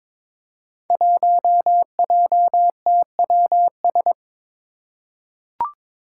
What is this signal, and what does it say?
Morse code, heard as a keyed tone, about 23 wpm, 700 Hz. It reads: 1JTWS
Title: Five Challenging Letter-Number Combinations 22wpm